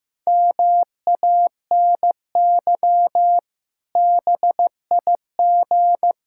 Morse code, heard as a keyed tone, about 15 wpm, 700 Hz. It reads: MANY BIG
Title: Pangram Challenge v2 - 15wpm